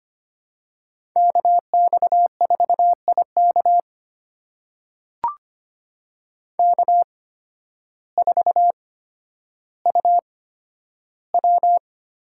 Morse code, heard as a keyed tone, about 25 wpm, spaced as for 12 wpm, 700 Hz. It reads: KX4IK K4UW
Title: USA Call Signs - 5 Characters - 8x Character Spacing - 25wpm